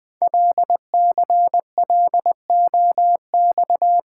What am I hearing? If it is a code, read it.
Morse code, heard as a keyed tone, about 20 wpm, 700 Hz. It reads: LCLOX
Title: Code Groups - 5 Letter:Numbers - 20wpm